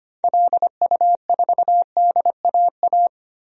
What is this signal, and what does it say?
Morse code, heard as a keyed tone, about 25 wpm, 700 Hz. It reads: LU4DAA
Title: DXCC-100 Call Signs - 6 Characters - Repeated 3x - 25wpm